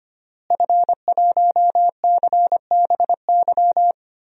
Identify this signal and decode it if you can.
Morse code, heard as a keyed tone, about 25 wpm, 700 Hz. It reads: F1CBY
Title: DXCC-100 Call Signs - 5 Characters - Repeated 3x - 25wpm